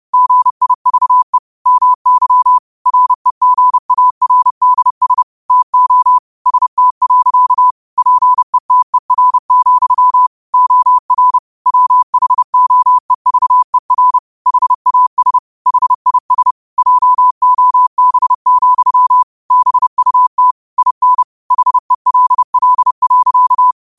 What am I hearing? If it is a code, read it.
Morse code, heard as a keyed tone, about 30 wpm, 1000 Hz. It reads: GIVE MY REGARDS TO ST. PETER, OR WHOEVER HAS HIS JOB, BUT IN HELL.
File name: qotdfiles / 30wpm